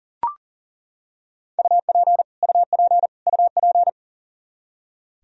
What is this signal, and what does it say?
Morse code, heard as a keyed tone, about 40 wpm, 700 Hz. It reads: UP UP UP
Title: Top 100 Words Repeated 40wpm